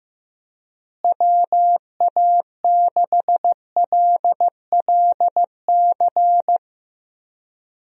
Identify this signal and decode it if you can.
Morse code, heard as a keyed tone, about 16 wpm, 700 Hz. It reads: WA6LLC
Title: USA Call Signs - 6 Characters - Repeated 3x - 15wpm